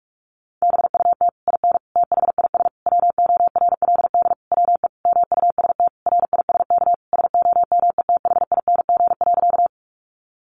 Morse code, heard as a keyed tone, about 44 wpm, 700 Hz. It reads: BUT IN THIS WORLD WE MUST RISK SOMETHING.
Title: Sentences from Top 600 Words 45wpm